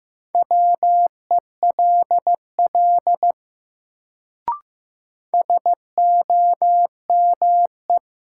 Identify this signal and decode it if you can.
Morse code, heard as a keyed tone, about 15 wpm, 700 Hz. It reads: WELL SOME
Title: Sets of 2 Words - New Word Drill - 15wpm